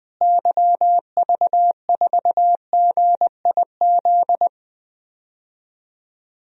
Morse code, heard as a keyed tone, about 21 wpm, 700 Hz. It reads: YV4GIZ
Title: DXCC-100 Call Signs - 6 Characters - Repeated 2x - 20wpm